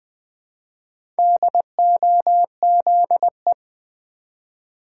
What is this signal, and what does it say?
Morse code, heard as a keyed tone, about 20 wpm, 700 Hz. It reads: DOZE
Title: Build A Word - 1 Letter At A Time - 5-Letter Words - 20wpm